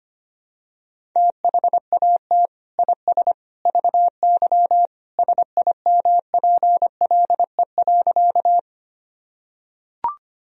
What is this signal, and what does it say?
Morse code, heard as a keyed tone, about 25 wpm, 700 Hz. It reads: THAT IS VY SIMPLE.
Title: Sentences from Top 400 Words and Abbreviations 25wpm